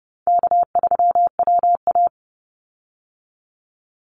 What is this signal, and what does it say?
Morse code, heard as a keyed tone, about 30 wpm, 700 Hz. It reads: K3WA
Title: Call Signs Repeated 2x - 30wpm